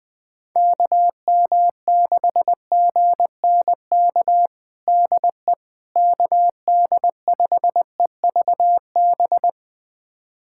Morse code, heard as a keyed tone, about 20 wpm, 700 Hz. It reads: KM6GNK DE KD5EVB
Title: USA Callsign DE Callsign - 6 Characters - 20wpm